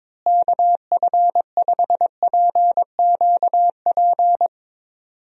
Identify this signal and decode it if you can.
Morse code, heard as a keyed tone, about 22 wpm, 700 Hz. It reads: KF5PQP
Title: USA Call Signs - 6 Characters - Repeated 2x - 22wpm